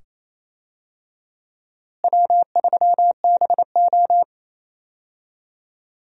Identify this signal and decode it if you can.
Morse code, heard as a keyed tone, about 28 wpm, 700 Hz. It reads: W3BO
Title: USA Call Signs - 4 Characters - Repeated 2x - 28wpm